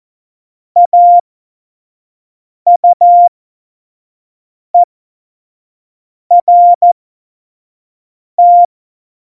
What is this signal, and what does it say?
Morse code, heard as a keyed tone, about 14 wpm, spaced as for 5 wpm, 700 Hz. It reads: AUERT
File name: Morsekurs / Witze / Witz 24 April 2018, 25 F70